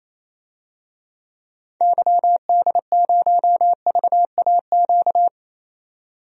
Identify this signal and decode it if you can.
Morse code, heard as a keyed tone, about 28 wpm, 700 Hz. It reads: YD0VAQ
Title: DXCC-100 Call Signs - 6 Characters - 28wpm